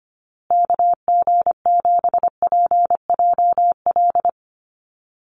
Morse code, heard as a keyed tone, about 25 wpm, 700 Hz. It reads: KG7PJL
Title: USA Call Signs Repeated 3x 25wpm